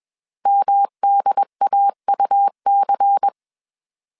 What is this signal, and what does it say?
Morse code, heard as a keyed tone, about 21 wpm, 800 Hz. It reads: MDAUC
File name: CW / DET / 1 6 1500C S B 20wpm 2048